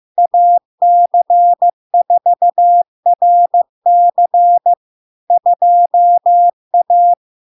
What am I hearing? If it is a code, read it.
Morse code, heard as a keyed tone, about 15 wpm, 700 Hz. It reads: AC4RC 2A
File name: LARK-FD-2022-Log-in-CW-15wpm-700-Hz